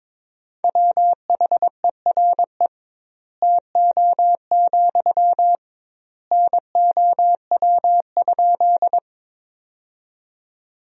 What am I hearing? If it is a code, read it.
Morse code, heard as a keyed tone, about 22 wpm, 700 Hz. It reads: WHERE TO, NOW?
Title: 2x Word-Spacing - Sentences from Top 200 Words - 22wpm